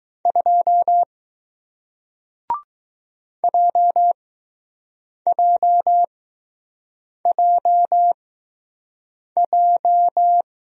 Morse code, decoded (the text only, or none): 2 JJJJ